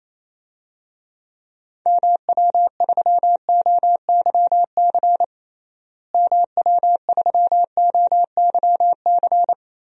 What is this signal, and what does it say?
Morse code, heard as a keyed tone, about 28 wpm, 700 Hz. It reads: MW3OYC MW3OYC
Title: DXCC-100 Call Signs - 6 Characters - Repeated 2x - 28wpm